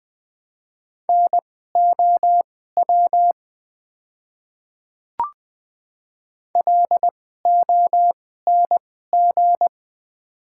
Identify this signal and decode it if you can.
Morse code, heard as a keyed tone, about 20 wpm, spaced as for 14 wpm, 700 Hz. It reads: NOW LONG
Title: 2x Character Spacing - Top 200 Words - 20wpm